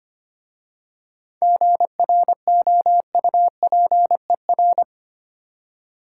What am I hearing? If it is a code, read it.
Morse code, heard as a keyed tone, about 25 wpm, 700 Hz. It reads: GROUPER
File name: practice / 25wpm / Difficult Words with Hints - 25wpm